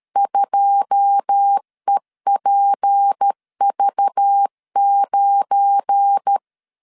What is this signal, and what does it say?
Morse code, heard as a keyed tone, about 13 wpm, 800 Hz. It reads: 2EPV9
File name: CW / DET / 1 9 1500C S C 12wpm 3811